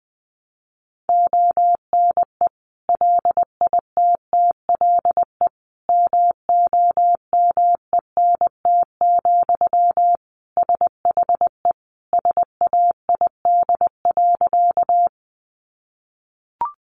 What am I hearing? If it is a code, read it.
Morse code, heard as a keyed tone, about 20 wpm, 700 Hz. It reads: ONE LITTLE MOMENT, SHE SAID.
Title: Sentences from Top 1000 Words 20wpm